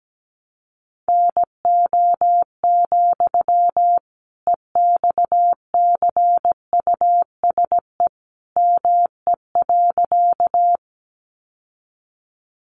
Morse code, heard as a keyed tone, about 17 wpm, 700 Hz. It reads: NO, EXCUSE ME.
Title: Sentences from Top 700 Words 17wpm